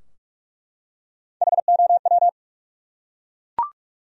Morse code, heard as a keyed tone, about 42 wpm, 700 Hz. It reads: HOW
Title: Top 100 Words 45wpm